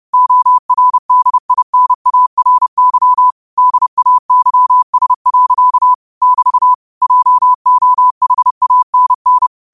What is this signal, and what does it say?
Morse code, heard as a keyed tone, about 30 wpm, 1000 Hz. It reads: ORDINARY DAYS. <BT> JOHANN
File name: qotdfiles / 30wpm